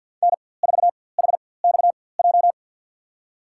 Morse code, heard as a keyed tone, about 52 wpm, spaced as for 23 wpm, 700 Hz. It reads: N4HXJ
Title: USA Call Signs - 5 Characters - 4x Character Spacing - 50wpm